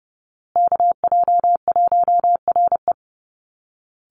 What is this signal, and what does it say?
Morse code, heard as a keyed tone, about 30 wpm, 700 Hz. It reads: KJ1RE
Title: Call Signs Repeated 3x - 30wpm